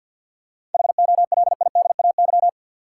Morse code, heard as a keyed tone, about 50 wpm, 700 Hz. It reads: HOLIDAY
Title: Seven Letter Words - 50wpm